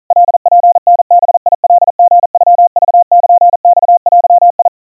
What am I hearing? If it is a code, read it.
Morse code (Japanese Wabun code), heard as a keyed tone, about 40 wpm, 700 Hz. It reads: カツタバカリノクルマデ
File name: kawa / Morse sample / kamyu 150 cpm 700 Hz Crispy